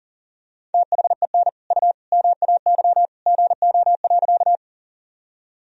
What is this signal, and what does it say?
Morse code, heard as a keyed tone, about 37 wpm, 700 Hz. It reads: THEN U MAY GO.
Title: Sentences from Top 100 Words and Abbreviations 40wpm